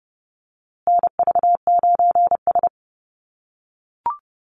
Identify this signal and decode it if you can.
Morse code, heard as a keyed tone, about 30 wpm, 700 Hz. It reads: NV9S
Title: USA Call Signs Repeated 3x 30wpm